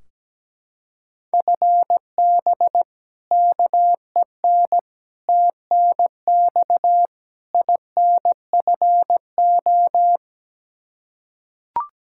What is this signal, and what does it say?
Morse code, heard as a keyed tone, about 17 wpm, 700 Hz. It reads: FB KEN TNX INFO
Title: Practice QSO - Exchange 3 - More Info - 17wpm